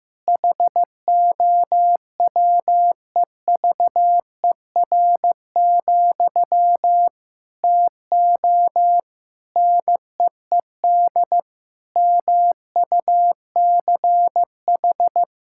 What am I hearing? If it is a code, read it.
Morse code, heard as a keyed tone, about 15 wpm, 700 Hz. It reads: HOWEVER, TO NEED MUCH
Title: Sentences from Top 1000 Words - Review Part 6 - 15wpm